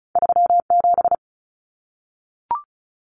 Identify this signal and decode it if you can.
Morse code, heard as a keyed tone, about 36 wpm, 700 Hz. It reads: 37
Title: Numbers - 2-Digit 35wpm